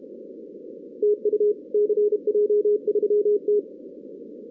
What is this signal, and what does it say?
Morse code, heard as a keyed tone, about 31 wpm, 420 Hz. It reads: TU CJ3T
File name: 2016 arrl dx cw / 2016-ARRL-DX-CW-K5TR- - 0-46